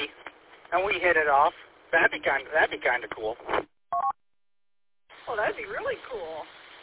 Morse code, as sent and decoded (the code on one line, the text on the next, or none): none